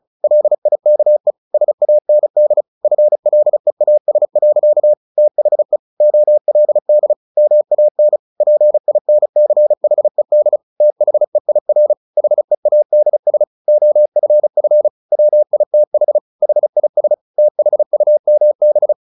Morse code, heard as a keyed tone, about 34 wpm, 600 Hz. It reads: LIKE SAND FLEAS. THE OLD MAN PINCHED THEIR HEADS OFF WITH HIS THUMB
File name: the old man and the sea 2 35wpm